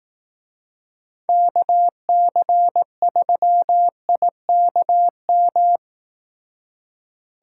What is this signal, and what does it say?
Morse code, heard as a keyed tone, about 18 wpm, 700 Hz. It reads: KC3IKM
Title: Speed Racing - USA Call Signs 15wpm